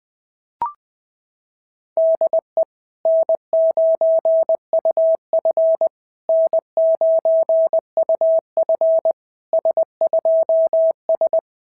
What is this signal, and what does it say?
Morse code, heard as a keyed tone, about 19 wpm, 650 Hz. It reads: DE N9UF N9UF S2S
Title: SOTA QSO Practice 20wpm